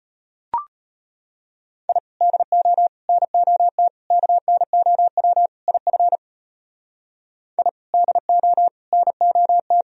Morse code, heard as a keyed tone, about 33 wpm, 700 Hz. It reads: I DO NOT KNOW IF I DO NOT